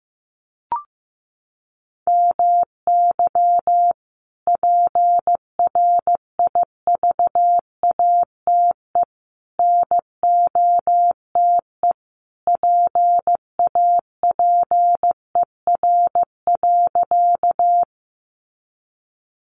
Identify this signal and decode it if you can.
Morse code, heard as a keyed tone, about 15 wpm, 700 Hz. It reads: MY PRIVATE NOTE PAPER.
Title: Sentences from Top 1000 Words 15wpm